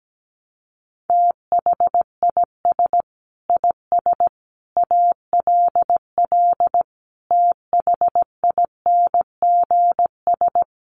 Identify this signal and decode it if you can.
Morse code, heard as a keyed tone, about 17 wpm, 700 Hz. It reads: THIS IS ALL THINGS
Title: Sets of 5 Words 17wpm